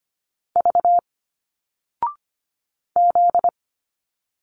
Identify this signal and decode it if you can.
Morse code, heard as a keyed tone, about 25 wpm, 700 Hz. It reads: V Z
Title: Single Letters 25wpm